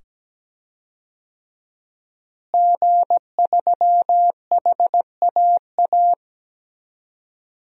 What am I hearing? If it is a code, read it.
Morse code, heard as a keyed tone, about 17 wpm, 700 Hz. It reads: G3HAA
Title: DXCC-100 Call Signs - 5 Characters - Repeated 2x - 17wpm